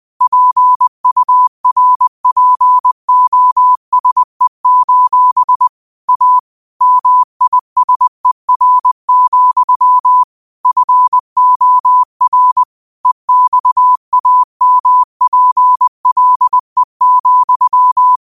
Morse code, heard as a keyed tone, about 20 wpm, 1000 Hz. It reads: PURPOSE: A MISER, FOR EXAMPLE,
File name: qotdfiles / 20wpm